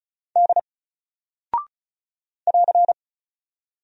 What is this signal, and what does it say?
Morse code, heard as a keyed tone, about 36 wpm, 700 Hz. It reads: D <AR>
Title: Single Character (Letter:Number:Punctuation:Prosign) 35wpm